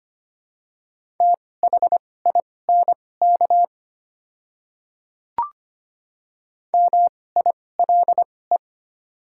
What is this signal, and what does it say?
Morse code, heard as a keyed tone, about 25 wpm, spaced as for 18 wpm, 700 Hz. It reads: THINK MILE
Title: 2x Character Spacing - Top 300 Words - 25wpm